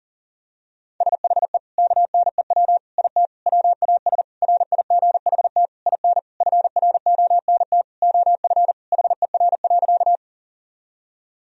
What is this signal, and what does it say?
Morse code, heard as a keyed tone, about 40 wpm, 700 Hz. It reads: SHE KNEW IT WAS RIGHT IN FRONT OF HER.
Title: Sentences from Top 1000 Words - Review Part 2 - 40wpm